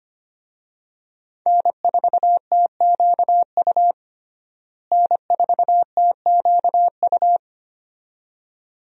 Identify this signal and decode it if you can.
Morse code, heard as a keyed tone, about 25 wpm, 700 Hz. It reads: N4TQU N4TQU
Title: USA Call Signs - 5 Characters - Repeated 2x - 25wpm